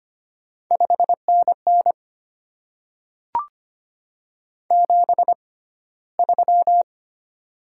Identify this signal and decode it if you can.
Morse code, heard as a keyed tone, about 25 wpm, spaced as for 15 wpm, 700 Hz. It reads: 5NN 73
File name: Top 20 QSO Elements - 6x Character Spacing - 25wpm